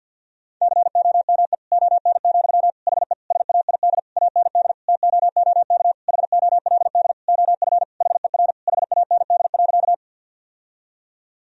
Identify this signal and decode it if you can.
Morse code, heard as a keyed tone, about 50 wpm, 700 Hz. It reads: COME ON, HE SAID AND TOOK HOLD OF HER HAND.
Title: Sentences from Top 1000 Words - Review Part 2 - 50wpm